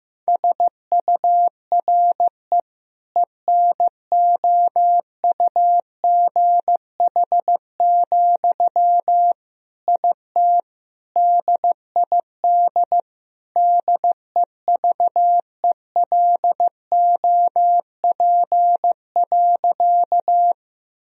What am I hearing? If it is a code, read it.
Morse code, heard as a keyed tone, about 15 wpm, 700 Hz. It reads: SURE ENOUGH, IT DID DEVELOP.
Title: Sentences from Top 1000 Words - Review Part 2 - 15wpm